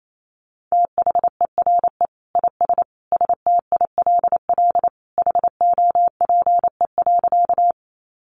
Morse code, heard as a keyed tone, about 28 wpm, 700 Hz. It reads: THERE IS STILL HOPE.
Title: Sentences from Top 1000 Words 28wpm